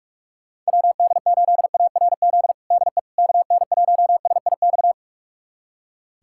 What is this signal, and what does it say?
Morse code, heard as a keyed tone, about 46 wpm, 700 Hz. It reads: WD8ARZ DE KN1SIX
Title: USA Callsign DE Callsign - 6 Characters - 45wpm